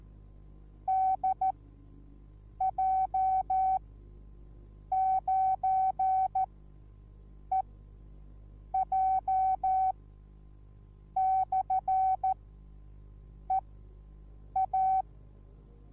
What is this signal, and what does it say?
Morse code, heard as a keyed tone, about 14 wpm, spaced as for 6 wpm, 760 Hz. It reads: DJ9EJ/EA